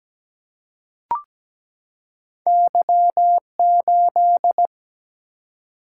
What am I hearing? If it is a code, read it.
Morse code, heard as a keyed tone, about 17 wpm, 700 Hz. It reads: Y8
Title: Single-Letter Single-Number 17wpm